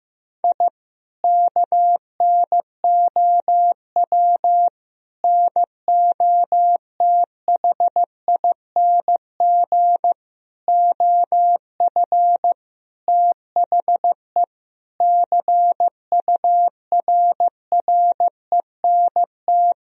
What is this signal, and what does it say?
Morse code, heard as a keyed tone, about 15 wpm, 700 Hz. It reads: I KNOW NOTHING OF THE CURRENT
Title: Sentences from Top 1000 Words - Review Part 4 - 15wpm